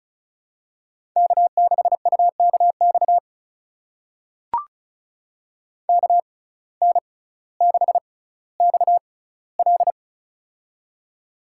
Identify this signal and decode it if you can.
Morse code, heard as a keyed tone, about 36 wpm, spaced as for 18 wpm, 700 Hz. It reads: K6UKX KN6XL
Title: USA Call Signs - 5 Characters - 6x Character Spacing - 35wpm